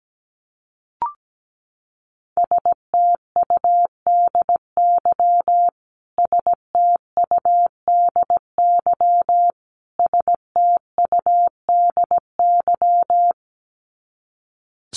Morse code, heard as a keyed tone, about 17 wpm, 700 Hz. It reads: STUDY STUDY STUDY
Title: Top 500 Words Repeated 17wpm